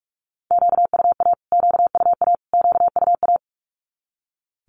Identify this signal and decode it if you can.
Morse code, heard as a keyed tone, about 44 wpm, 700 Hz. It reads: QUA QUA QUA
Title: Q-Codes Repeated 3x 45wpm